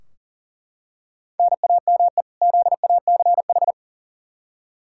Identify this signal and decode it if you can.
Morse code, heard as a keyed tone, about 40 wpm, 700 Hz. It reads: NAME ZACH
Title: K1USN SST QSO Practice 40wpm